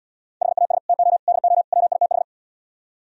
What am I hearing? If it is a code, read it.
Morse code, heard as a keyed tone, about 77 wpm, 700 Hz. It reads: HAS TO DO WITH